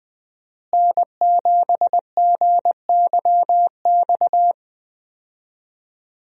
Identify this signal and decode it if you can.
Morse code, heard as a keyed tone, about 20 wpm, 700 Hz. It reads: N7GYX
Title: USA Call Signs - 5 Characters - Repeated 3x - 20wpm